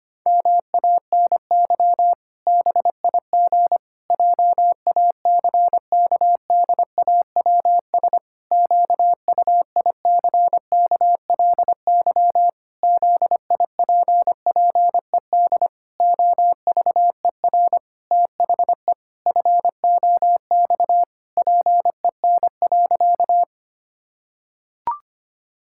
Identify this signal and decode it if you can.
Morse code, heard as a keyed tone, about 25 wpm, 700 Hz. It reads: MANY BIG JACKDAWS QUICKLY ZIPPED OVER THE FOX PEN.
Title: Pangram Challenge v2 - 25wpm